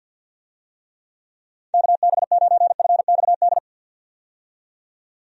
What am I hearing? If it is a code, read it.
Morse code, heard as a keyed tone, about 51 wpm, 700 Hz. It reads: KB9FXD
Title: USA Call Signs - 6 Characters - 50wpm